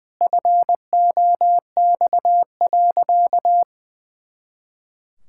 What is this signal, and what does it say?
Morse code, heard as a keyed tone, about 20 wpm, 700 Hz. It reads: FOX.
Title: Pangram Challenge v2 - 20wpm